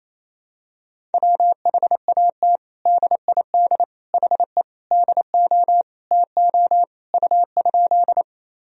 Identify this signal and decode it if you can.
Morse code, heard as a keyed tone, about 28 wpm, 700 Hz. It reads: WHAT DID HE DO TO U?
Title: Sentences from Top 100 Words and Abbreviations 28wpm